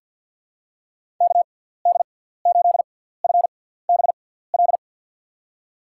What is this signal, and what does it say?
Morse code, heard as a keyed tone, about 53 wpm, spaced as for 17 wpm, 700 Hz. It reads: KD8FBL
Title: USA Call Signs - 6 Characters - 6x Character Spacing - 50wpm